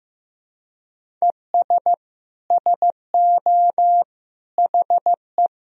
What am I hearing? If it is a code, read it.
Morse code, heard as a keyed tone, about 15 wpm, 700 Hz. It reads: ES SO HE
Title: Sentences from Top 500 Words and Abbreviations 15wpm